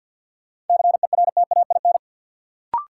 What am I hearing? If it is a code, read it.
Morse code, heard as a keyed tone, about 51 wpm, 700 Hz. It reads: CERTAIN
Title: Sentences from Top 500 Words - New Word Drill - 50wpm